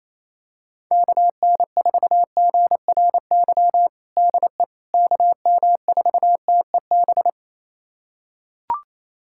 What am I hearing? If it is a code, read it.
Morse code, heard as a keyed tone, about 28 wpm, 700 Hz. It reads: KN4GRY DE KM4TEB
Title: USA Callsign DE Callsign 28wpm